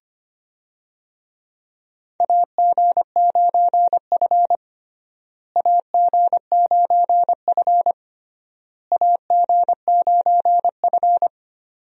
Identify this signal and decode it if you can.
Morse code, heard as a keyed tone, about 25 wpm, 700 Hz. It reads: AG9F AG9F AG9F
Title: USA Call Signs - 4 Characters - Repeated 3x - 25wpm